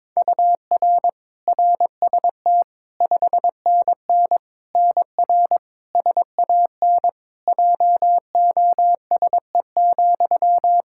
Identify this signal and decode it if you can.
Morse code, heard as a keyed tone, about 22 wpm, 700 Hz. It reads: UR RST 5NN NR SAN JOSE,
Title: Practice QSO - Exchange 1 - Beginning - 22wpm